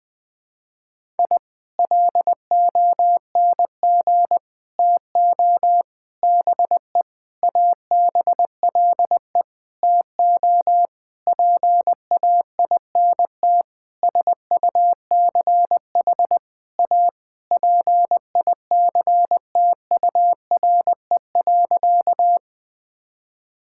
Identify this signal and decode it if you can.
Morse code, heard as a keyed tone, about 20 wpm, 700 Hz. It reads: I LONG TO BE ABLE TO PAINT SUCH A PICTURE.
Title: Sentences from Top 1000 Words - Review Part 6 - 20wpm